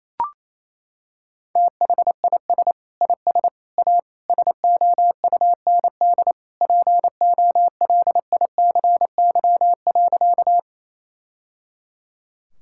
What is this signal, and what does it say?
Morse code, heard as a keyed tone, about 28 wpm, 700 Hz. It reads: THIS IS A SOUND POLICY.
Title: Sentences from Top 1000 Words - Review Part 5 - 28wpm